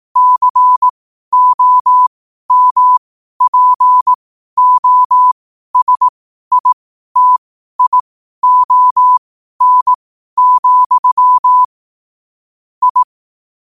Morse code, heard as a keyed tone, about 18 wpm, spaced as for 12 wpm, 1000 Hz. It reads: COMPOSITION, I
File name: 12wpm